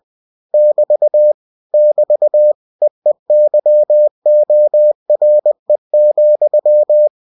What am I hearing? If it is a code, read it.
Morse code, heard as a keyed tone, about 20 wpm, 600 Hz. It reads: <BT> <BT> EEYORE,
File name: cw-books / winnie-the-pooh / winnie-20wpm-page-0044